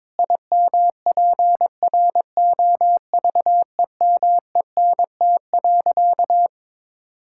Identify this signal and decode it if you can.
Morse code, heard as a keyed tone, about 22 wpm, 700 Hz. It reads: IMPROVEMENT.
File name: practice / 22wpm / Sentences with 8-Word Phrases - 22wpm